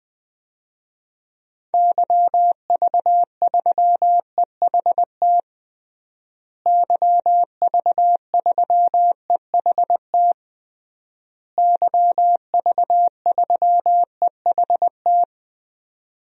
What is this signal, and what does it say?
Morse code, heard as a keyed tone, about 21 wpm, 700 Hz. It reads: YV3EHT YV3EHT YV3EHT
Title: DXCC-100 Call Signs - 6 Characters - Repeated 3x - 20wpm